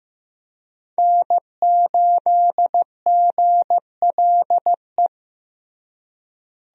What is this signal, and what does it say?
Morse code, heard as a keyed tone, about 15 wpm, 700 Hz. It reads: N8GLE
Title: USA Call Signs - 5 Characters - Repeated 2x - 15wpm